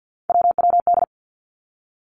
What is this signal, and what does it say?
Morse code, heard as a keyed tone, about 50 wpm, 700 Hz. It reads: WWD
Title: Three Challenging Letter-Number Combinations 50wpm